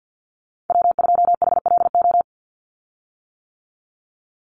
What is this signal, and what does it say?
Morse code, heard as a keyed tone, about 50 wpm, 700 Hz. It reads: W2HLO
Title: USA Call Signs 50wpm